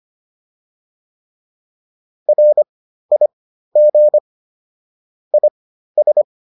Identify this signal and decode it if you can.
Morse code, heard as a keyed tone, about 25 wpm, spaced as for 13 wpm, 600 Hz. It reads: RIG IS
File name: QSO310 13